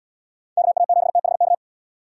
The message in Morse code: KA7THK

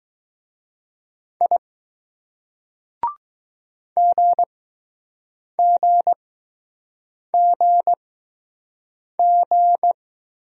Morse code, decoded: I GGGG

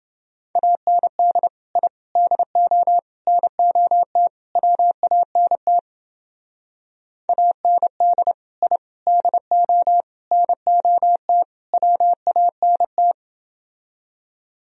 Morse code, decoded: AND I DO NOT WANT AND I DO NOT WANT

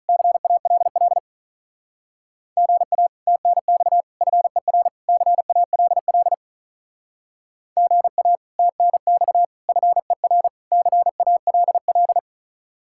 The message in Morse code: CALL GA TNX FER CALL GA TNX FER CALL

